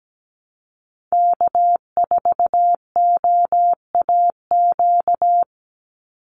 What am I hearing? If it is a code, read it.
Morse code, heard as a keyed tone, about 17 wpm, 700 Hz. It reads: K4OAQ
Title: Call Signs Repeated 3x - 17wpm